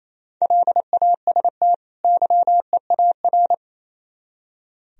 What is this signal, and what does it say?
Morse code, heard as a keyed tone, about 37 wpm, 700 Hz. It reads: LAST YEAR